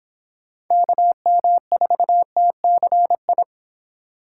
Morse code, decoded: KM4TCI

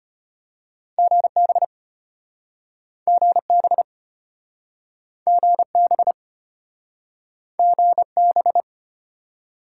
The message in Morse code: GB GB GB GB

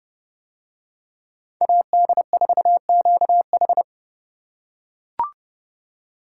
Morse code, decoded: AD4QH

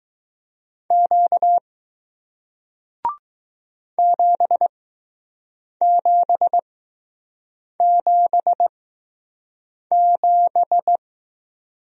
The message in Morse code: Q 7777

